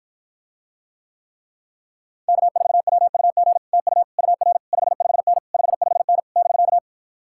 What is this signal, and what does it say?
Morse code, heard as a keyed tone, about 52 wpm, 700 Hz. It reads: K4WUG TU UR 55N 55N <BK>